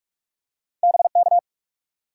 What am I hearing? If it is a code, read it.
Morse code, heard as a keyed tone, about 42 wpm, 700 Hz. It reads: BK
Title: Top 90 QSO Elements 45wpm